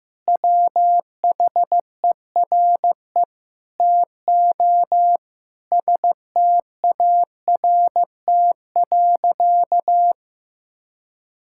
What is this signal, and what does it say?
Morse code, heard as a keyed tone, about 15 wpm, 700 Hz. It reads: WHERE TO START.
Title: Sentences from Top 1000 Words - Review Part 5 - 15wpm